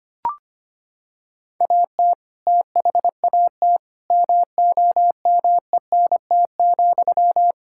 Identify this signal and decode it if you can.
Morse code, heard as a keyed tone, about 25 wpm, 700 Hz. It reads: AT THAT MOMENT,
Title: Sentences from Top 1600 Words 25wpm